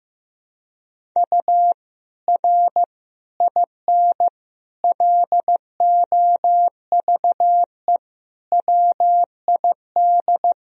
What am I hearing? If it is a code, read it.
Morse code, heard as a keyed tone, about 15 wpm, 700 Hz. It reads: U R IN LOVE WID
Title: Sentences from Top 400 Words and Abbreviations 15wpm